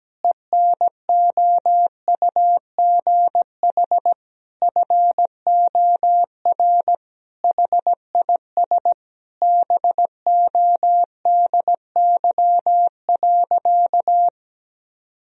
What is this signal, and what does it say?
Morse code, heard as a keyed tone, about 17 wpm, 700 Hz. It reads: ENOUGH FOR HIS BODY.
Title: Sentences from Top 1000 Words - Review Part 2 - 17wpm